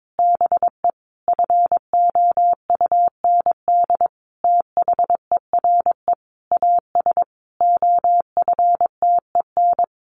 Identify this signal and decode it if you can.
Morse code, heard as a keyed tone, about 22 wpm, 700 Hz. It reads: BE FOUND THERE AS OFTEN
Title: Sentences from Top 1400 Words 22wpm